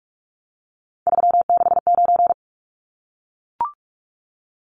Morse code, heard as a keyed tone, about 44 wpm, 700 Hz. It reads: 369
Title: Numbers - 3-Digit 45wpm